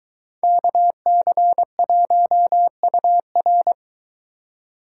Morse code, heard as a keyed tone, about 22 wpm, 700 Hz. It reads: KC1UR